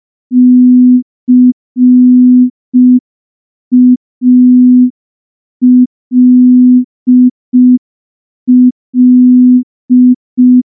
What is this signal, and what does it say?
Morse code, heard as a keyed tone, about 5 wpm, 250 Hz. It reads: CALL
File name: morse / morse-training-05WPM